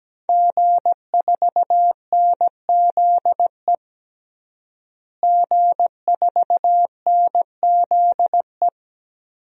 Morse code, heard as a keyed tone, about 17 wpm, 700 Hz. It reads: G4NZE G4NZE
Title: DXCC-100 Call Signs - 5 Characters - Repeated 2x - 17wpm